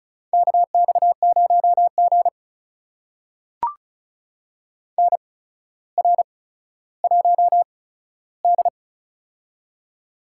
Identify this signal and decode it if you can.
Morse code, heard as a keyed tone, about 36 wpm, spaced as for 15 wpm, 700 Hz. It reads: KX0G NR1D
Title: USA Call Signs - 4 Characters - 8x Character Spacing - 35wpm